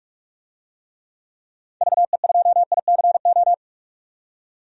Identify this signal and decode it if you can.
Morse code, heard as a keyed tone, about 46 wpm, 700 Hz. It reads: VE2ICO
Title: DXCC-100 Call Signs - 6 Characters - 45wpm